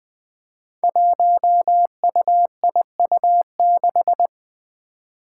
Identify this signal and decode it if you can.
Morse code, heard as a keyed tone, about 20 wpm, 700 Hz. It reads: 1UIU6
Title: Five Challenging Letter-Number Combinations 20wpm